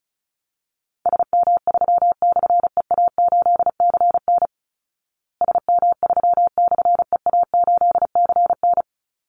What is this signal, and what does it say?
Morse code, heard as a keyed tone, about 34 wpm, 700 Hz. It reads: SM3/EA8CN SM3/EA8CN
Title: Call Signs Repeated 2x - 35wpm